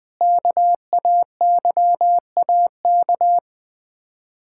Morse code, heard as a keyed tone, about 21 wpm, 700 Hz. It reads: KAYAK
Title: Difficult Words - Speed-Racing with Hints - 20wpm